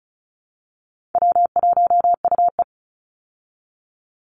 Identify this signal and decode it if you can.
Morse code, heard as a keyed tone, about 34 wpm, 700 Hz. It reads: W1UE
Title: Easy Call Signs Repeated 3x - 35wpm